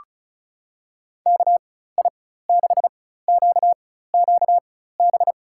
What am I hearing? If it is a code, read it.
Morse code, heard as a keyed tone, about 36 wpm, spaced as for 16 wpm, 690 Hz. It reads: KI6QQB
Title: USA Call Signs - 6 Characters - 4x Character Spacing - 35wpm